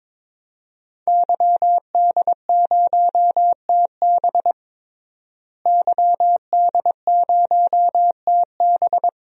Morse code, heard as a keyed tone, about 22 wpm, 700 Hz. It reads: YD0TB YD0TB
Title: DXCC-100 Call Signs - 5 Characters - Repeated 2x - 22wpm